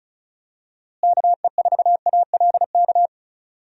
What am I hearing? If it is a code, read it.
Morse code, heard as a keyed tone, about 36 wpm, 700 Hz. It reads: KE4ALK
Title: USA Call Signs - 6 Characters - Repeated 3x - 35wpm